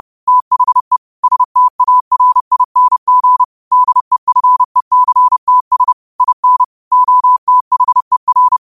Morse code, heard as a keyed tone, about 30 wpm, 1000 Hz. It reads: THE STARING DEFECTS IN OTHER
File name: qotdfiles / 30wpm